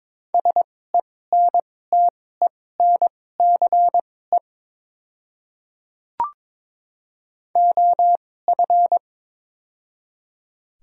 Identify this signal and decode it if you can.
Morse code, heard as a keyed tone, about 21 wpm, spaced as for 16 wpm, 700 Hz. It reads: SENTENCE OF
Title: 2x Character Spacing - Top 300 Words - 22wpm